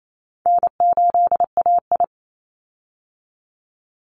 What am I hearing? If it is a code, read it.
Morse code, heard as a keyed tone, about 28 wpm, 700 Hz. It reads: N8AI
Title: Call Signs Repeated 3x - 28wpm